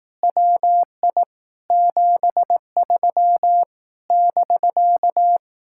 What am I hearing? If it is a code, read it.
Morse code, heard as a keyed tone, about 18 wpm, 700 Hz. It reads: WI 73 <BK>
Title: Speed-Racing Common QSO Phrases Encore - 15wpm